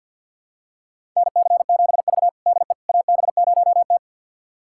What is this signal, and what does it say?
Morse code, heard as a keyed tone, about 51 wpm, 700 Hz. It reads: NC7V DE AB0T
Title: USA Callsign DE Callsign - 4 Characters - 50wpm